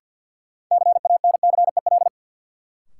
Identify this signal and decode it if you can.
Morse code, heard as a keyed tone, about 53 wpm, 700 Hz. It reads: CANCEL
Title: Sentences from Top 1200 Words - New Word Drill - 50wpm